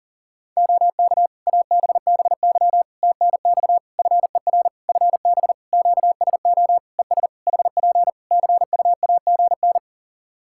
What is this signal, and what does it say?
Morse code, heard as a keyed tone, about 40 wpm, 700 Hz. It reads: OK ABBY TNX FER FB QSO ES HP CUAGN
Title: Practice QSO - Exchange 4 - Ending - 40wpm